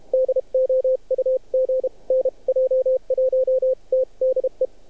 Morse code, heard as a keyed tone, about 30 wpm, 520 Hz. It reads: DOUG NJ1T DE
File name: archived / CRSnet